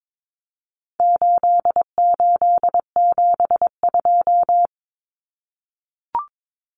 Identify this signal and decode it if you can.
Morse code, heard as a keyed tone, about 22 wpm, 700 Hz. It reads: 8872
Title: Numbers - 4-Digit 22wpm